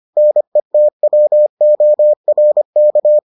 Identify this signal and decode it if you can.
Morse code, heard as a keyed tone, about 25 wpm, 600 Hz. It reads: NETWORK